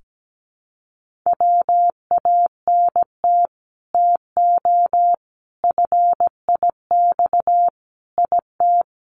Morse code, heard as a keyed tone, about 17 wpm, 700 Hz. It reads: WANT TO FIX IT
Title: Sentences from Top 1400 Words 17wpm